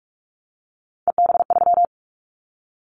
Morse code, heard as a keyed tone, about 46 wpm, 700 Hz. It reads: EB3
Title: Callsign Prefixes for Top 25 DXCC - 45wpm